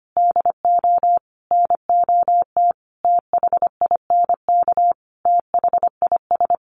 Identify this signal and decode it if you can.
Morse code, heard as a keyed tone, about 25 wpm, 700 Hz. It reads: DO NOT THINK THIS
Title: Sets of 5 Words Encore 25wpm